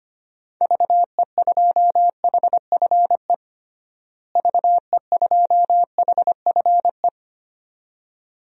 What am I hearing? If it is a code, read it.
Morse code, heard as a keyed tone, about 25 wpm, 700 Hz. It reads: VE2HFE VE2HFE
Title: DXCC-100 Call Signs - 6 Characters - Repeated 2x - 25wpm